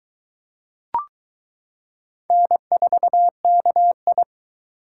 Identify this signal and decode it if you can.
Morse code, heard as a keyed tone, about 22 wpm, 700 Hz. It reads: N4KI